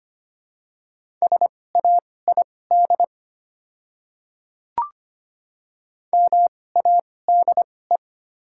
Morse code, heard as a keyed tone, about 25 wpm, spaced as for 18 wpm, 700 Hz. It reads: SAID MADE
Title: 2x Character Spacing - Top 300 Words - 25wpm